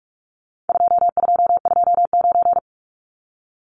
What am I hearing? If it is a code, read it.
Morse code, heard as a keyed tone, about 44 wpm, 700 Hz. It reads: 2229
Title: Numbers - 4-Digit 45wpm